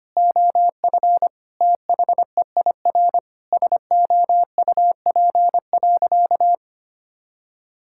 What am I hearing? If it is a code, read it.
Morse code, heard as a keyed tone, about 25 wpm, 700 Hz. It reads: OF THEIR SOUP.
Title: Sentences from Top 1000 Words - Review Part 6 - 25wpm